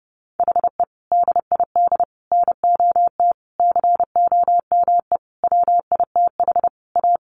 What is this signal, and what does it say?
Morse code, heard as a keyed tone, about 30 wpm, 700 Hz. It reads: HE DID NOT COME WITH A
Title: Sentences from Top 1200 Words 30wpm